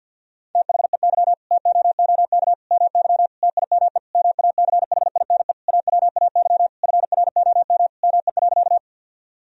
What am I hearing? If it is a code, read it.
Morse code, heard as a keyed tone, about 52 wpm, 700 Hz. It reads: THEY TOOK MY TIME MACHINE AWAY FROM ME.
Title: Sentences from Top 1000 Words - Review Part 2 - 50wpm